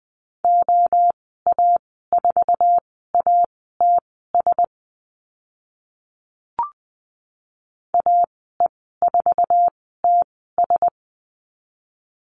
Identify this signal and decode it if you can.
Morse code, heard as a keyed tone, about 20 wpm, spaced as for 14 wpm, 700 Hz. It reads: OA4ATS AE4TS
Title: 069 Lesson 12g - Letter d - Call Signs 20wpm